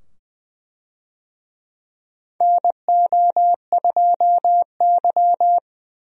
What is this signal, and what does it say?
Morse code, heard as a keyed tone, about 20 wpm, 700 Hz. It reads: NO2Y
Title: USA Call Signs - 4 Characters - 20wpm